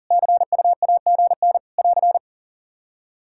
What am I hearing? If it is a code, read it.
Morse code, heard as a keyed tone, about 42 wpm, 700 Hz. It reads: CU AGN <AR>
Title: Practice QSO - Exchange 4 - Ending - 40wpm